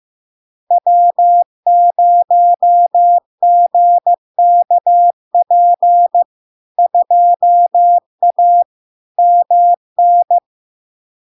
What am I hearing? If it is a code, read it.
Morse code, heard as a keyed tone, about 15 wpm, 700 Hz. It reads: W0GKP 2A MN
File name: LARK-FD-2022-Log-in-CW-15wpm-700-Hz